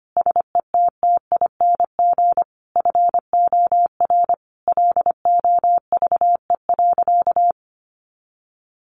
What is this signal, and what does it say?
Morse code, heard as a keyed tone, about 25 wpm, 700 Hz. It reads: SETTING FOR LOVE.
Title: Sentences from Top 1300 Words 25wpm